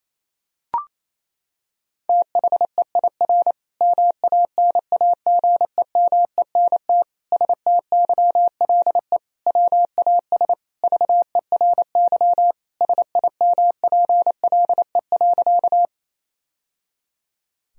Morse code, heard as a keyed tone, about 28 wpm, 700 Hz. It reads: THEIR MANAGEMENT STYLE WAS VERY SIMPLE.
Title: Sentences from Top 1000 Words - Review Part 4 - 28wpm